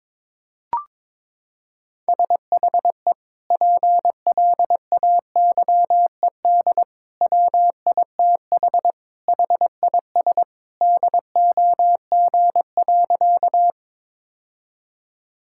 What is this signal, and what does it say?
Morse code, heard as a keyed tone, about 22 wpm, 700 Hz. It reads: SHE PLAYED WITH HIS DOG.
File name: practice / 22wpm / Sentences with 2-Word Phrases - 22wpm